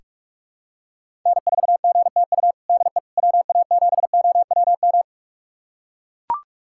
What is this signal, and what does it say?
Morse code, heard as a keyed tone, about 41 wpm, 700 Hz. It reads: N4GTU DE WA7OWM
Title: USA Callsign DE Callsign 45wpm